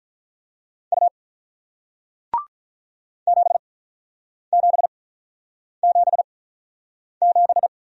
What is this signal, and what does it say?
Morse code, heard as a keyed tone, about 48 wpm, spaced as for 9 wpm, 700 Hz. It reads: U 7777